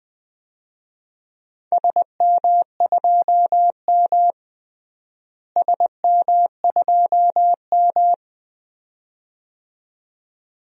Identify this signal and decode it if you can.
Morse code, heard as a keyed tone, about 20 wpm, 700 Hz. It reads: SM2M SM2M
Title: DXCC-100 Call Signs - 4 Characters - Repeated 2x - 20wpm